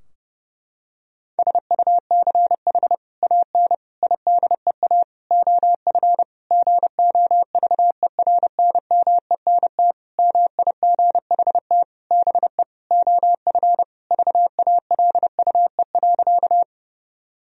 Morse code, heard as a keyed tone, about 32 wpm, 700 Hz. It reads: SUCH AN IDEA OF GOVERNMENT MIGHT BE OF VALUE.
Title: Sentences from Top 1000 Words - Review Part 3 - 30wpm